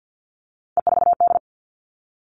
Morse code, heard as a keyed tone, about 50 wpm, 700 Hz. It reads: E4D
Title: All Cut Numbers - 3-Digit 50wpm